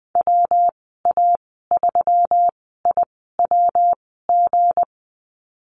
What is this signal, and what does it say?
Morse code, heard as a keyed tone, about 20 wpm, spaced as for 14 wpm, 700 Hz. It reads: WA3IWG
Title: 179 Lesson 29g - Slash - Call Signs 20wpm